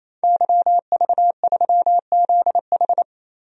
Morse code, heard as a keyed tone, about 28 wpm, 700 Hz. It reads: YV3ZH
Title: DXCC-100 Call Signs - 5 Characters - 28wpm